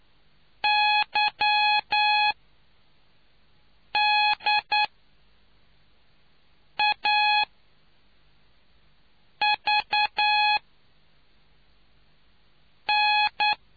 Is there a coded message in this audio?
Yes: Morse code from 0.6 to 13.5 s, heard as a keyed tone, about 9 wpm, spaced as for 4 wpm, 840 Hz. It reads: YDAVN